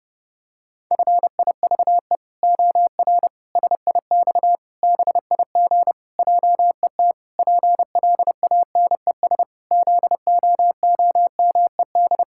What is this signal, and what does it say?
Morse code, heard as a keyed tone, about 32 wpm, 700 Hz. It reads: FIVE OR SIX BIG JET PLANES ZOOMED
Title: Pangram Challenge v2 - 30wpm